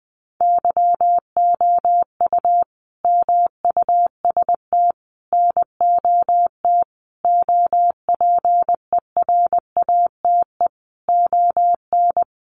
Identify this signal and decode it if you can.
Morse code, heard as a keyed tone, about 20 wpm, 700 Hz. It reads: YOU MUST NOT OPERATE ON
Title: Sentences from Top 1500 Words 20wpm